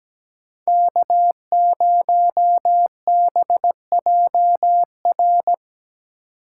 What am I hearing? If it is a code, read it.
Morse code, heard as a keyed tone, about 17 wpm, 700 Hz. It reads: K0BJR
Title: USA Call Signs - 5 Characters - 3x Character Spacing - 17wpm